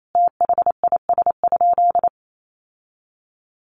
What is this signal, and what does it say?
Morse code, heard as a keyed tone, about 28 wpm, 700 Hz. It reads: THIS?
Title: Sentences from Top 1300 Words 28wpm